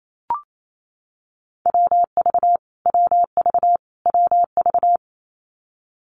Morse code, heard as a keyed tone, about 28 wpm, 700 Hz. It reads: WV WV WV
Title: US States Repeated 3x 28wpm